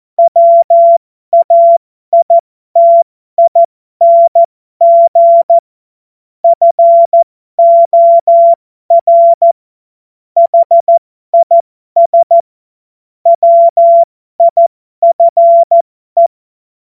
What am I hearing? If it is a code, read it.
Morse code, heard as a keyed tone, about 14 wpm, spaced as for 12 wpm, 680 Hz. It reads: WAITING FOR HIS WIFE
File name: Morsekurs / Witze